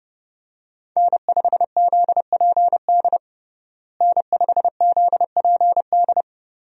Morse code, heard as a keyed tone, about 32 wpm, 700 Hz. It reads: N5ZPD N5ZPD
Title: USA Call Signs - 5 Characters - Repeated 2x - 30wpm